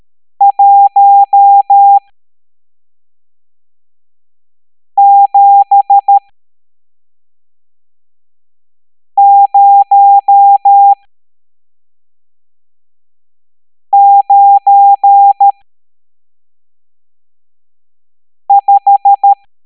Morse code, heard as a keyed tone, about 13 wpm, spaced as for 3 wpm, 800 Hz. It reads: 17095
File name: hamdata / L12-EX02-CHIFFRES